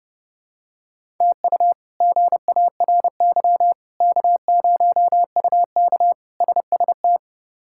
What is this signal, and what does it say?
Morse code, heard as a keyed tone, about 30 wpm, 700 Hz. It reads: TU GARY K0UK SST
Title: K1USN SST QSO Practice - 30wpm